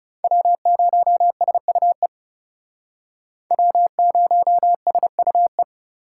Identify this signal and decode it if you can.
Morse code, heard as a keyed tone, about 37 wpm, 700 Hz. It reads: W0SUE W0SUE
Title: Speed Racing - USA 1x3 Call Signs 30wpm